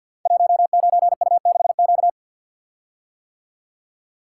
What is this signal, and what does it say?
Morse code, heard as a keyed tone, about 52 wpm, 700 Hz. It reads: 19U6Q
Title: Code Groups - 5 Letter:Numbers - 50wpm